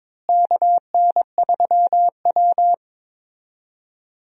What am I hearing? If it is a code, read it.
Morse code, heard as a keyed tone, about 22 wpm, 700 Hz. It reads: KN3W
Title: USA Call Signs - 4 Characters - Repeated 2x - 22wpm